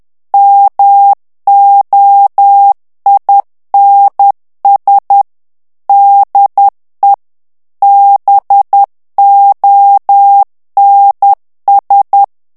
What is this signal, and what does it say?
Morse code, heard as a keyed tone, about 10 wpm, 800 Hz. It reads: MOINS DE BONS
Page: Français - texte en clair 1 - vitesse : 10 mots minute